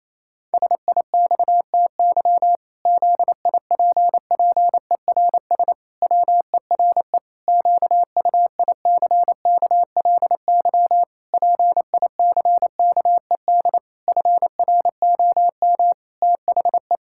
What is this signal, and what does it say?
Morse code, heard as a keyed tone, about 28 wpm, 700 Hz. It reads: SIXTY ZIPPERS WERE QUICKLY PICKED FROM THE
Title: Pangram Challenge v2 - 28wpm